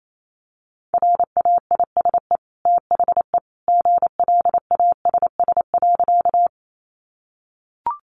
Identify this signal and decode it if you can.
Morse code, heard as a keyed tone, about 28 wpm, 700 Hz. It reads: RAISE THE GLASS.
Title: Sentences from Top 1400 Words 28wpm